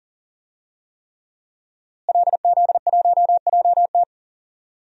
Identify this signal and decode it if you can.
Morse code, heard as a keyed tone, about 40 wpm, 700 Hz. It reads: LZ1JT
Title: DXCC-100 Call Signs - 5 Characters - Repeated 3x - 40wpm